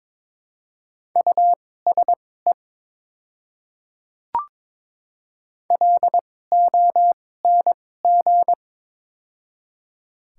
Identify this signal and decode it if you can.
Morse code, heard as a keyed tone, about 22 wpm, spaced as for 16 wpm, 690 Hz. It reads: USE LONG
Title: 2x Character Spacing - Top 100 Words - 22wpm